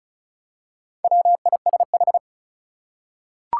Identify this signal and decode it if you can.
Morse code, heard as a keyed tone, about 36 wpm, 700 Hz. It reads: WISH
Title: Four Letter Words - 35wpm